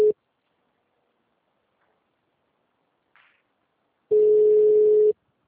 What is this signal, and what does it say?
Ringback tone (Europe, 425 Hz).